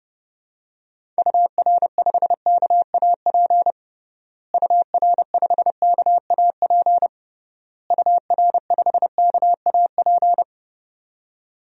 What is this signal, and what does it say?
Morse code, heard as a keyed tone, about 30 wpm, 700 Hz. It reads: UR5KAP UR5KAP UR5KAP
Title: DXCC-100 Call Signs - 6 Characters - Repeated 3x - 30wpm